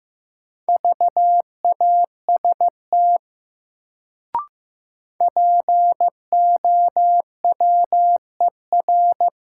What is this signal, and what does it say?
Morse code, heard as a keyed tone, about 15 wpm, 700 Hz. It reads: VAST POWER
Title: Sets of 4 Words - New Word Drill - 15wpm